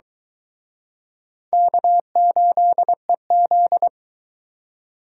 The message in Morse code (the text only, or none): K8EZ